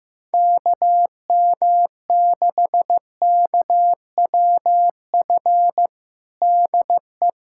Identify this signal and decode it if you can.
Morse code, heard as a keyed tone, about 15 wpm, 700 Hz. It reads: KM6KWF DE
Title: USA Callsign DE Callsign - 6 Characters - 15wpm